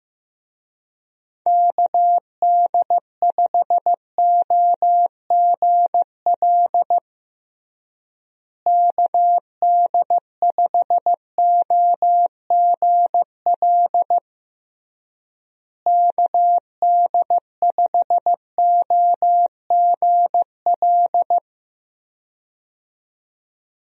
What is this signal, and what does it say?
Morse code, heard as a keyed tone, about 15 wpm, 700 Hz. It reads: KD5OGL KD5OGL KD5OGL
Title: USA Call Signs - 6 Characters - Repeated 3x - 15wpm